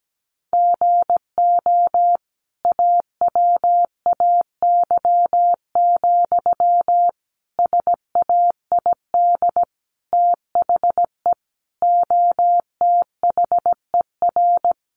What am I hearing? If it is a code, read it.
Morse code, heard as a keyed tone, about 17 wpm, 700 Hz. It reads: GO AWAY, SAID THE OTHER
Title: Sentences from Top 1500 Words 17wpm